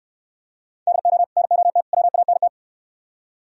Morse code, heard as a keyed tone, about 64 wpm, 700 Hz. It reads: DO NOT WANT